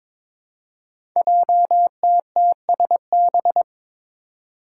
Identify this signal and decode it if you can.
Morse code, heard as a keyed tone, about 22 wpm, 700 Hz. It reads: JTTSB
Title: Five Challenging Letter-Number Combinations 22wpm